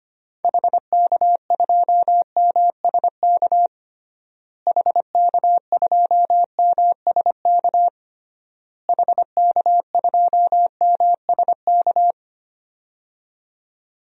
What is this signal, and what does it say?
Morse code, heard as a keyed tone, about 25 wpm, 700 Hz. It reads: HK2MSK HK2MSK HK2MSK
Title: DXCC-100 Call Signs - 6 Characters - Repeated 3x - 25wpm